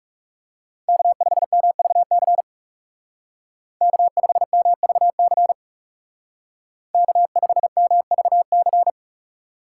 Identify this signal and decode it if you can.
Morse code, heard as a keyed tone, about 35 wpm, 700 Hz. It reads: K5MVC K5MVC K5MVC
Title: Speed Racing - USA Call Signs 30wpm